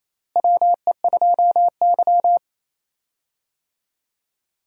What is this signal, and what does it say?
Morse code, heard as a keyed tone, about 28 wpm, 700 Hz. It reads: WE2Y
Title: USA Call Signs - 4 Characters - Repeated 2x - 28wpm